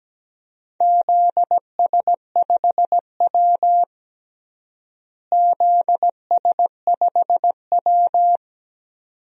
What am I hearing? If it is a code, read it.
Morse code, heard as a keyed tone, about 17 wpm, 700 Hz. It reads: ZS5W ZS5W
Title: DXCC-100 Call Signs - 4 Characters - Repeated 2x - 17wpm